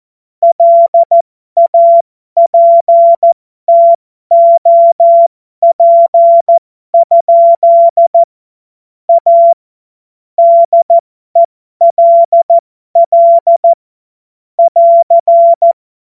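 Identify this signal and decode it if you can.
Morse code, heard as a keyed tone, about 14 wpm, spaced as for 12 wpm, 680 Hz. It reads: LAPTOP? A DELL <AR>
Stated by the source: Morse code joke practice.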